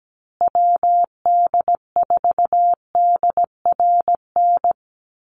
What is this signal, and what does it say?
Morse code, heard as a keyed tone, about 17 wpm, 700 Hz. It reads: WD4DRN
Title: USA Call Signs Repeated 2x 17wpm